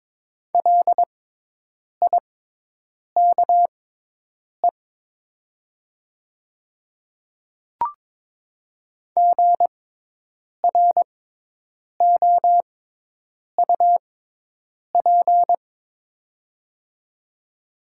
Morse code, heard as a keyed tone, about 22 wpm, spaced as for 8 wpm, 700 Hz. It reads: LIKE GROUP
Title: 6x Character Spacing - Top 300 Words - 22wpm